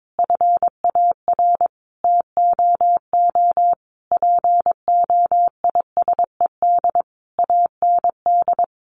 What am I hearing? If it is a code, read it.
Morse code, heard as a keyed tone, about 22 wpm, 700 Hz. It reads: FAR TOO POISED AND
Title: A Sound Of Thunder 22wpm